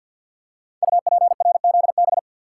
Morse code, heard as a keyed tone, about 53 wpm, 700 Hz. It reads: UPRZB
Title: Code Groups - 5 Letters - 50wpm